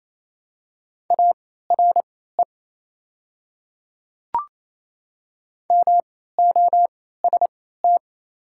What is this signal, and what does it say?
Morse code, heard as a keyed tone, about 28 wpm, spaced as for 16 wpm, 700 Hz. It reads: ARE MOST
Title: 3x Character Spacing - Top 100 Words - 28wpm